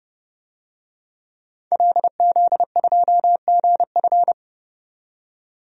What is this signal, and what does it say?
Morse code, heard as a keyed tone, about 30 wpm, 700 Hz. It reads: LZ2GF
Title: DXCC-100 Call Signs - 5 Characters - 30wpm